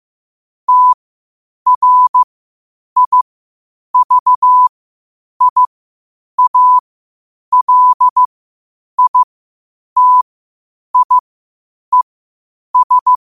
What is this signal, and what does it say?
Morse code, heard as a keyed tone, about 15 wpm, spaced as for 8 wpm, 1000 Hz. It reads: TRIVIALITIES
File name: qotdfiles / 8wpm